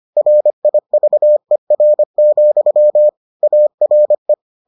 Morse code, heard as a keyed tone, about 25 wpm, 600 Hz. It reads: RIVER, ARE